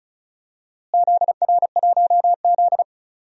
Morse code, heard as a keyed tone, about 34 wpm, 700 Hz. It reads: ZR1Z
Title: DXCC-100 Call Signs - 4 Characters - Repeated 3x - 35wpm